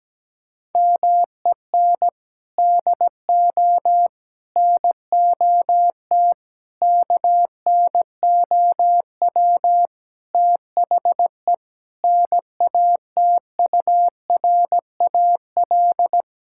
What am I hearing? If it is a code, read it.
Morse code, heard as a keyed tone, about 17 wpm, 700 Hz. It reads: MEN DO NOT KNOW THE NATURAL
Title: Sentences from Top 1000 Words - Review Part 6 - 17wpm